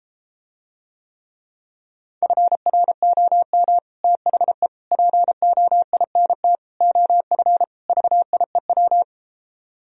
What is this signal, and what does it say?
Morse code, heard as a keyed tone, about 33 wpm, 700 Hz. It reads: FROM THE POINT OF VIEW